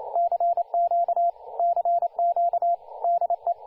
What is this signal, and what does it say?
Morse code, heard as a keyed tone, about 29 wpm, 690 Hz, with radio noise. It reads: CQ CQ DE
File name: du1-ja1pbv-1-28-040mhz-cw-cqing